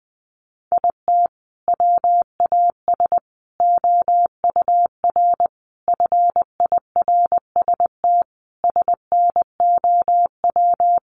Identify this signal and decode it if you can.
Morse code, heard as a keyed tone, about 20 wpm, 700 Hz. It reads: IT WAS OUR FIRST SNOW
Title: Sentences from Top 1500 Words 20wpm